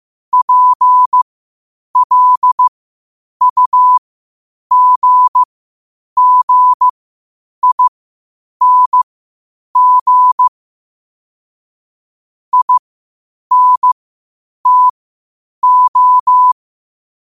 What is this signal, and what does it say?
Morse code, heard as a keyed tone, about 15 wpm, spaced as for 8 wpm, 1000 Hz. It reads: PLUGGING INTO
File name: qotdfiles / 8wpm